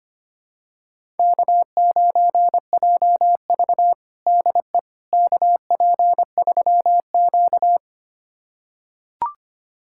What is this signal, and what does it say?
Morse code, heard as a keyed tone, about 25 wpm, 700 Hz. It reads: K9JV DE KP3Q
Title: USA Callsign DE Callsign - 4 Characters - 25wpm